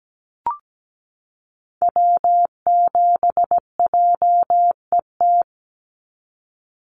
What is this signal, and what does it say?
Morse code, heard as a keyed tone, about 17 wpm, 700 Hz. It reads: W7JET
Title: NA SOTA Activator Callsigns 17wpm